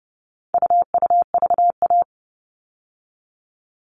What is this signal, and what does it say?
Morse code, heard as a keyed tone, about 30 wpm, 700 Hz. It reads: UUVA
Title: All Cut Numbers - 4-Digit 30wpm